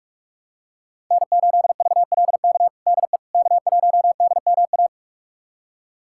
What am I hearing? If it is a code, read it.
Morse code, heard as a keyed tone, about 46 wpm, 700 Hz. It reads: N8VLK DE K1DMA
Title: USA Callsign DE Callsign - 5 Characters - 45wpm